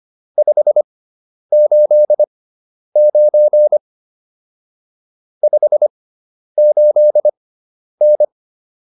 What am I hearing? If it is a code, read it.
Morse code, heard as a keyed tone, about 25 wpm, spaced as for 10 wpm, 600 Hz. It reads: 589 58N